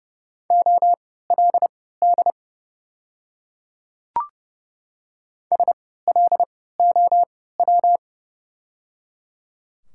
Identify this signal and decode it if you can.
Morse code, heard as a keyed tone, about 30 wpm, spaced as for 17 wpm, 700 Hz. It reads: OLD SLOW
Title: 3x Character Spacing - Top 500 Words - 30wpm